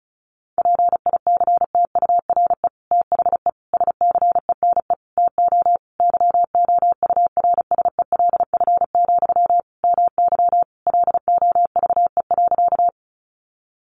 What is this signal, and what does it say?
Morse code, heard as a keyed tone, about 34 wpm, 700 Hz. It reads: PICTURE THE SCENE TO YOURSELF, MY LOVE.
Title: Sentences from Top 1100 Words 35wpm